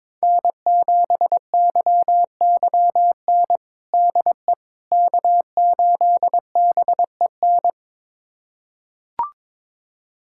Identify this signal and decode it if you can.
Morse code, heard as a keyed tone, about 22 wpm, 700 Hz. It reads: N7YYN DE K8BEN
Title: USA Callsign DE Callsign - 5 Characters - 22wpm